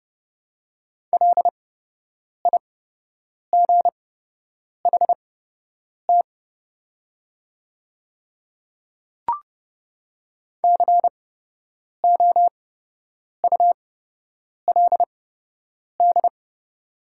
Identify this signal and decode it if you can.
Morse code, heard as a keyed tone, about 30 wpm, spaced as for 8 wpm, 700 Hz. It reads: LIGHT COULD
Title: 8x Character Spacing - Top 300 Words - 30wpm